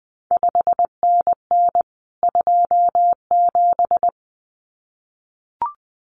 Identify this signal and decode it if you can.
Morse code, heard as a keyed tone, about 20 wpm, 700 Hz. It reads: 5NN 27
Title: CQ WW DX Contest Exchange Encore - 20wpm